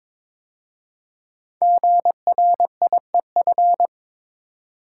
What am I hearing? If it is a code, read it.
Morse code, heard as a keyed tone, about 22 wpm, 690 Hz. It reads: GRIEF
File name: 5 Letter Words - First 3 Letters Given - 22wpm